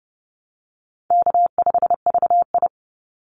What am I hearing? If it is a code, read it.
Morse code, heard as a keyed tone, about 30 wpm, 700 Hz. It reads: K5VI
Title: Call Signs Repeated 2x - 30wpm